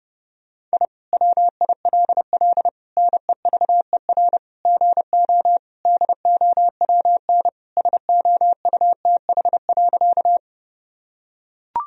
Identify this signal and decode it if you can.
Morse code, heard as a keyed tone, about 30 wpm, 700 Hz. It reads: I WILL NEVER GO DOWN SOUTH.
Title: Sentences from Top 1000 Words - Review Part 2 - 30wpm